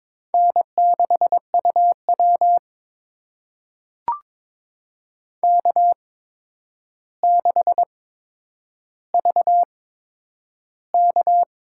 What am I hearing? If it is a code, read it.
Morse code, heard as a keyed tone, about 21 wpm, spaced as for 9 wpm, 700 Hz. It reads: N6UW K6VK
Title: USA Call Signs - 4 Characters - 8x Character Spacing - 22wpm